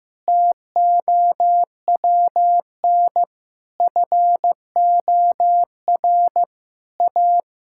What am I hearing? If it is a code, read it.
Morse code, heard as a keyed tone, about 16 wpm, 700 Hz. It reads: TOWN FOR A
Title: Sentences from Top 1000 Words - Review Part 5 - 15wpm